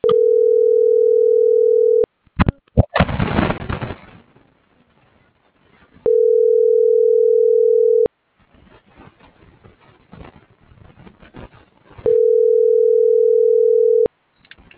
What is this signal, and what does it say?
Ringback tone (North America), 440 + 480 Hz.